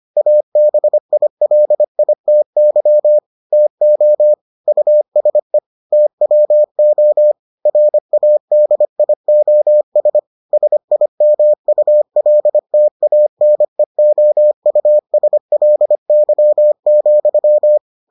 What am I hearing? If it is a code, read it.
Morse code, heard as a keyed tone, about 25 wpm, 600 Hz. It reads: ABILITY TO USE TWO RADIOS SIMULTANEOUSLY,